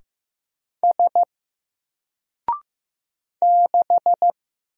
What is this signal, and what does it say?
Morse code, heard as a keyed tone, about 15 wpm, 700 Hz. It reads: S 6
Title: Single Character (Letter:Number:Punctuation:Prosign) 15wpm